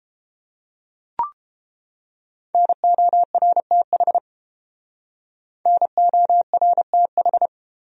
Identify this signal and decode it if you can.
Morse code, heard as a keyed tone, about 29 wpm, 700 Hz. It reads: NORTH NORTH